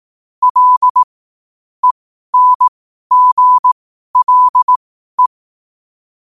Morse code, heard as a keyed tone, about 18 wpm, spaced as for 12 wpm, 1000 Hz. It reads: L ENGLE
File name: qotdfiles / 12wpm